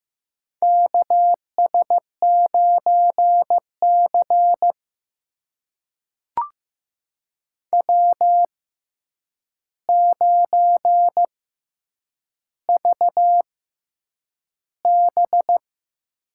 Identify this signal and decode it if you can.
Morse code, heard as a keyed tone, about 15 wpm, spaced as for 8 wpm, 700 Hz. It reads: KS9C W9VB
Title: USA Call Signs - 4 Characters - 6x Character Spacing - 15wpm